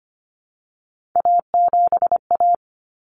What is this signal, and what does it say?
Morse code, heard as a keyed tone, about 25 wpm, 700 Hz. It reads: A7A
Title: Common Cut Numbers - 3-Digit 25wpm